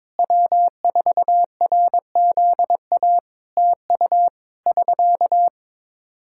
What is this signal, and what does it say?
Morse code, heard as a keyed tone, about 22 wpm, 700 Hz. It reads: W4RZA TU <SK>
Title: Practice QSO - Exchange 4 - Ending - 22wpm